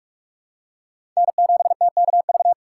Morse code, heard as a keyed tone, about 46 wpm, 700 Hz. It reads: N7TKV
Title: USA Call Signs - 4x Character Spacing - 45wpm